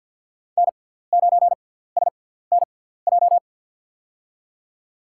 Morse code, heard as a keyed tone, about 49 wpm, spaced as for 17 wpm, 700 Hz. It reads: N9SNJ
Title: USA Call Signs - 5 Characters - 6x Character Spacing - 50wpm